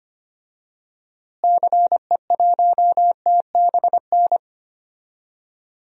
Morse code, heard as a keyed tone, about 25 wpm, 700 Hz. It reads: CE1TBN
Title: DXCC-100 Call Signs - 6 Characters - Repeated 3x - 25wpm